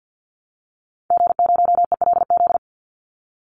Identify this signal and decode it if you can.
Morse code, heard as a keyed tone, about 50 wpm, 700 Hz. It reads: G0ELZ
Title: Call Signs - 50wpm